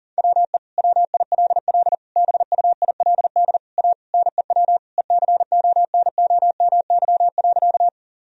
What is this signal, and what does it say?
Morse code, heard as a keyed tone, about 37 wpm, 700 Hz. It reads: WE WILL BUILD A NEW ECONOMY.
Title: Sentences from Top 1000 Words - Review Part 3 - 40wpm